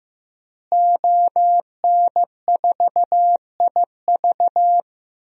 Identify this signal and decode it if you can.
Morse code, heard as a keyed tone, about 15 wpm, 700 Hz. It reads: ON4IV
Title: DXCC-100 Call Signs - 5 Characters - Repeated 3x - 15wpm